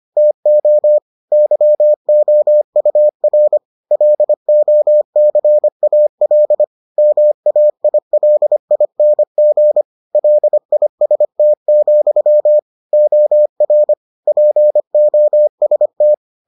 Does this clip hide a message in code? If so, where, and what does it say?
Yes: Morse code from 0.2 to 16.1 s, heard as a keyed tone, about 25 wpm, 600 Hz. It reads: TO YOUR LOCAL MAILING LIST, OR POST